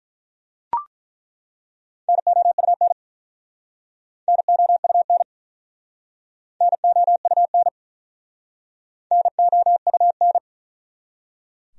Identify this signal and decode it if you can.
Morse code, heard as a keyed tone, about 48 wpm, 700 Hz. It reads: NOUN NOUN NOUN NOUN